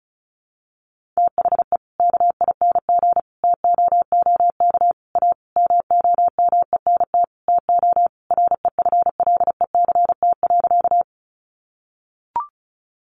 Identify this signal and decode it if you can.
Morse code, heard as a keyed tone, about 34 wpm, 700 Hz. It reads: THE KING TOOK A MOMENT TO REFLECT.
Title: Sentences from Top 1500 Words 35wpm